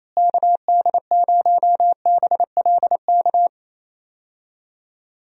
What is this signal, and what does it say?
Morse code, heard as a keyed tone, about 28 wpm, 700 Hz. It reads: KD0BLK
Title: USA Call Signs - 6 Characters - Repeated 3x - 28wpm